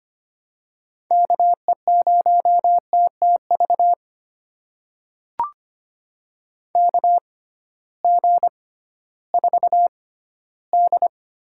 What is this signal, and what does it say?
Morse code, heard as a keyed tone, about 25 wpm, spaced as for 14 wpm, 700 Hz. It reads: KE0TTV KG4D
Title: USA Call Signs - 6x Character Spacing - 25wpm